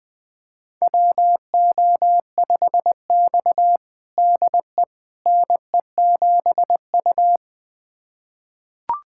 Morse code, heard as a keyed tone, about 20 wpm, 700 Hz. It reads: WO5X DE NE7U
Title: USA Callsign DE Callsign - 4 Characters - 20wpm